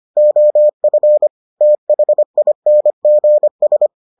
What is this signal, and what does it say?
Morse code, heard as a keyed tone, about 25 wpm, 600 Hz. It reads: OF THINGS